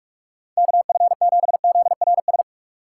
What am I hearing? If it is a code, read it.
Morse code, heard as a keyed tone, about 42 wpm, 700 Hz. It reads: KF7ZRS
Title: USA Call Signs - 6 Characters - 3x Character Spacing - 45wpm